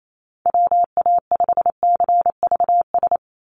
Morse code, heard as a keyed tone, about 28 wpm, 700 Hz. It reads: WA5CVS
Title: USA Call Signs 28wpm